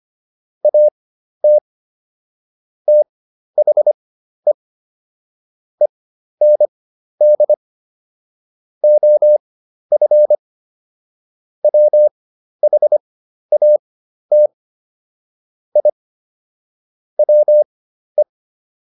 Morse code, heard as a keyed tone, about 25 wpm, spaced as for 12 wpm, 600 Hz. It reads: AT THE END OF WHAT I WE